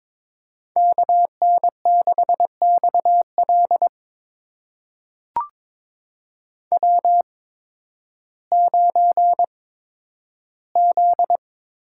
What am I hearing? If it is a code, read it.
Morse code, heard as a keyed tone, about 22 wpm, spaced as for 12 wpm, 700 Hz. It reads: KN6XL W9Z
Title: USA Call Signs - 5 Characters - 8x Character Spacing - 22wpm